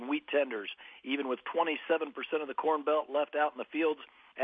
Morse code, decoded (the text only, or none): none